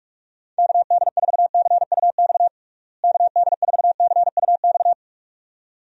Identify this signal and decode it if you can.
Morse code, heard as a keyed tone, about 41 wpm, 700 Hz. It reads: KD4CUX KD4CUX
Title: USA Call Signs - 6 Characters - Repeated 2x - 45wpm